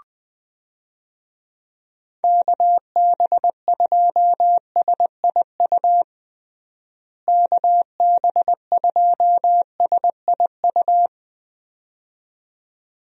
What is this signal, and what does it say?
Morse code, heard as a keyed tone, about 20 wpm, 700 Hz. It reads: KB2SIU KB2SIU
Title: USA Call Signs - 6 Characters - Repeated 2x - 20wpm